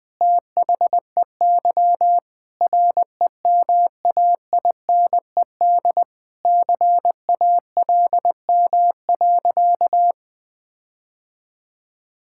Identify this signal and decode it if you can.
Morse code, heard as a keyed tone, about 21 wpm, 700 Hz. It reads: THEY REMAINED CALM.
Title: Sentences with 5-Word Phrases - Spoken Beginning - 20wpm